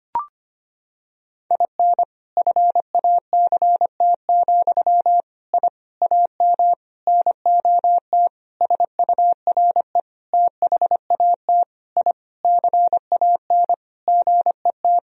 Morse code, heard as a keyed tone, about 25 wpm, 700 Hz. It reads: IN FACT, I AM NOT SURE THAT I CAN GET
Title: Sentences from Top 1000 Words - Review Part 3 - 25wpm